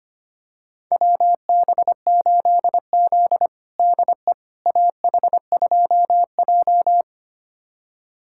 Morse code, heard as a keyed tone, about 25 wpm, 700 Hz. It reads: WB8Z DE AH2J
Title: USA Callsign DE Callsign - 4 Characters - 25wpm